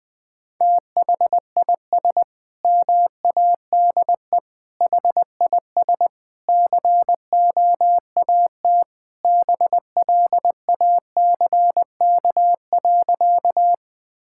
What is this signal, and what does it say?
Morse code, heard as a keyed tone, about 20 wpm, 700 Hz. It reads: THIS MADE HIS COAT BLACK.
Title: Sentences from Top 1600 Words 20wpm